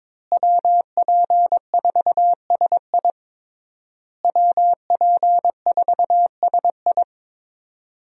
Morse code, heard as a keyed tone, about 22 wpm, 700 Hz. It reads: WP4SI WP4SI
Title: USA Call Signs - 5 Characters - Repeated 3x - 22wpm